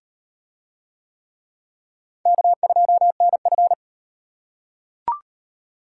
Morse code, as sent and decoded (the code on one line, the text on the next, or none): -.- ..--- -. ..-.
K2NF